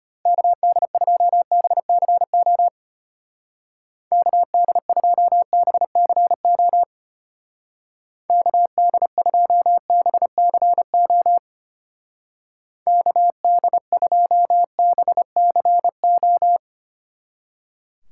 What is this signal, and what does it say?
Morse code, heard as a keyed tone, about 33 wpm, 700 Hz. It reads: KD2BCO KD2BCO KD2BCO KD2BCO